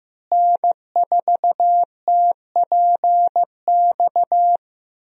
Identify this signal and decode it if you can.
Morse code, heard as a keyed tone, about 15 wpm, 700 Hz. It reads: N4TPX
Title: USA Call Signs - 5 Characters - Repeated 3x - 15wpm